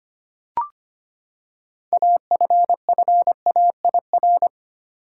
Morse code, heard as a keyed tone, about 25 wpm, 700 Hz. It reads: AFFAIR
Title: Six Letter Words - 25wpm